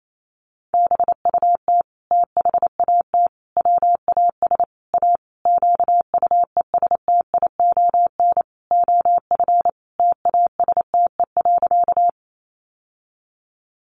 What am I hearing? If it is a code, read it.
Morse code, heard as a keyed tone, about 28 wpm, 700 Hz. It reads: BUT THAT WAS A QUESTION OF TASTE.
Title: Sentences from Top 1100 Words 28wpm